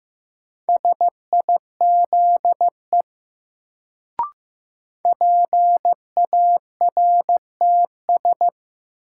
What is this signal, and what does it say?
Morse code, heard as a keyed tone, about 15 wpm, 700 Hz. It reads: SIZE PARTS
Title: Sets of 3 Words - New Word Drill - 15wpm